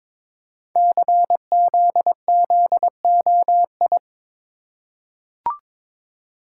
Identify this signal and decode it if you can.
Morse code, heard as a keyed tone, about 22 wpm, 700 Hz. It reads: CZZOI
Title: Code Groups - 5 Letters - 22wpm